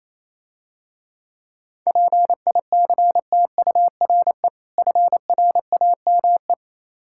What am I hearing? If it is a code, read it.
Morse code, heard as a keyed tone, about 27 wpm, 690 Hz. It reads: PICTURE FRAME
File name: Easy Words - Speed-Racing with Hints - 22wpm